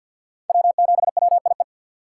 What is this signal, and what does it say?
Morse code, heard as a keyed tone, about 53 wpm, 690 Hz. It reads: W7WIE
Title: USA Call Signs - 4x Character Spacing - 50wpm